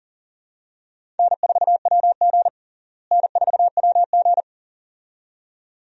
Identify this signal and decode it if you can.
Morse code, heard as a keyed tone, about 40 wpm, 700 Hz. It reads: N4WG N4WG
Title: USA Call Signs - 4 Characters - Repeated 2x - 40wpm